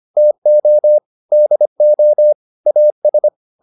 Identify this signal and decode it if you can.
Morse code, heard as a keyed tone, about 25 wpm, 600 Hz. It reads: TO DO AS